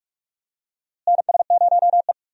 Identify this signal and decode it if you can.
Morse code, heard as a keyed tone, about 47 wpm, 700 Hz. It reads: NS0E
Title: USA Call Signs - 4 Characters - Repeated 2x - 45wpm